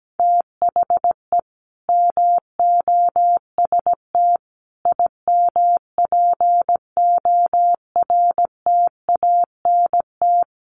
Morse code, heard as a keyed tone, about 17 wpm, 700 Hz. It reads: THE MOST IMPORTANT
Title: Sets of 4 Words 17wpm